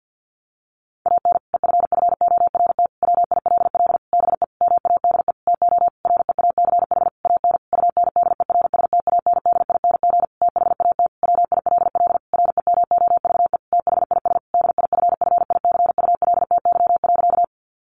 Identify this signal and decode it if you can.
Morse code, heard as a keyed tone, about 50 wpm, 700 Hz. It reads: AN EFFORT WILL BE MADE TO REACH AN UNDERSTANDING THAT WILL REMOVE THIS DIFFICULTY.
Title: Sentences from Top 1500 Words 50wpm